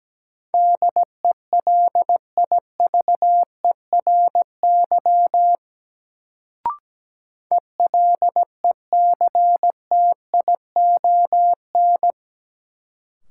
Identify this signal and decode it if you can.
Morse code, heard as a keyed tone, about 17 wpm, 700 Hz. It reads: DELIVERY ELECTION
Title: Sentences from Top 1500 Words - New Word Drill Alt - 17wpm